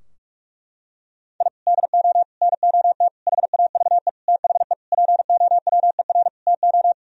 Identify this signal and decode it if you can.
Morse code, heard as a keyed tone, about 45 wpm, 700 Hz. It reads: I DO NOT HAVE THE POWER TO
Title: Sentences from Top 1000 Words - Review Part 3 - 45wpm